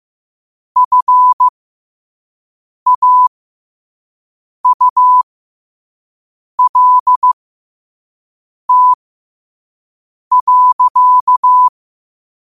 Morse code, heard as a keyed tone, about 15 wpm, spaced as for 5 wpm, 1000 Hz. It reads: FAULT.
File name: 5wpm